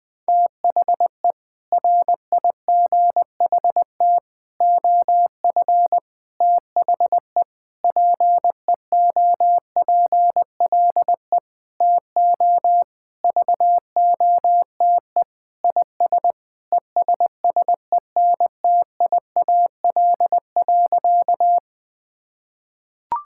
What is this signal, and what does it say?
Morse code, heard as a keyed tone, about 21 wpm, 700 Hz. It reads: THE RIGHT OF THE PEOPLE TO VOTE IS ESSENTIAL.
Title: Sentences with 6-Word Phrases - 20wpm